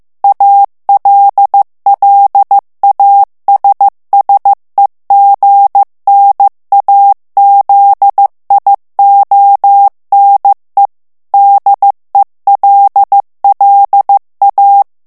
Morse code, heard as a keyed tone, about 15 wpm, 800 Hz. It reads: ALLASSEGNAZIONE DELLA
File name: morse mp3 / ITALIEN texte clair 2 vitesse 14 mots minute